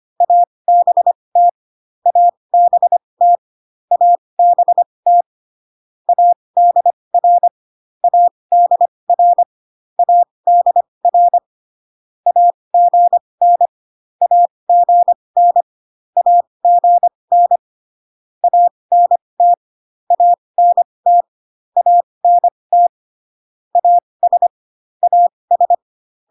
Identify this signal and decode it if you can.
Morse code, heard as a keyed tone, about 25 wpm, spaced as for 20 wpm, 700 Hz. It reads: ABT ABT ABT ADR ADR ADR AGN AGN AGN ANT ANT ANT AS AS